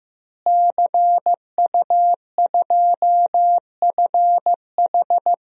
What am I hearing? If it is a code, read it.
Morse code, heard as a keyed tone, about 16 wpm, 700 Hz. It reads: CU2FH
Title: DXCC-100 Call Signs - 5 Characters - Repeated 3x - 15wpm